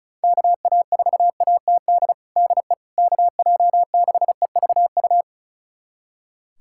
Morse code, heard as a keyed tone, about 34 wpm, 700 Hz. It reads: KA4ATD DE KJ6EVU
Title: USA Callsign DE Callsign - 6 Characters - 35wpm